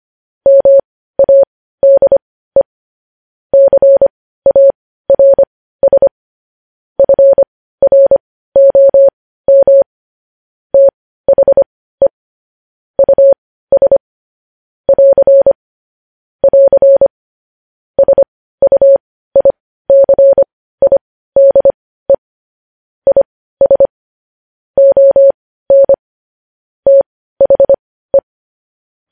Morse code, heard as a keyed tone, about 25 wpm, spaced as for 15 wpm, 550 Hz. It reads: MADE CARS FROM THE US <AR> <AR> SUICIDE IS ON THE